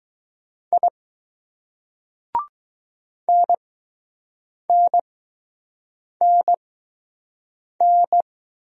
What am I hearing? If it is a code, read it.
Morse code, heard as a keyed tone, about 20 wpm, spaced as for 6 wpm, 700 Hz. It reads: I NNNN